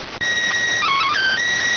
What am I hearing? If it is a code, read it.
SSTV (slow-scan television), PD 290.